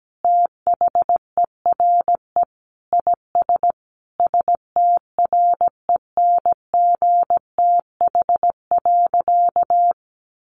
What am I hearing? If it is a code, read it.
Morse code, heard as a keyed tone, about 17 wpm, 700 Hz. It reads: THERE IS STRENGTH.
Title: Sentences from Top 1400 Words 17wpm